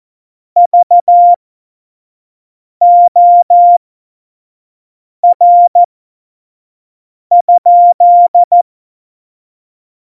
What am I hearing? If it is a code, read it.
Morse code, heard as a keyed tone, about 14 wpm, spaced as for 5 wpm, 700 Hz. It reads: VOR?
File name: Morsekurs / Witze / Witz 21. Mai 2019, 25 F70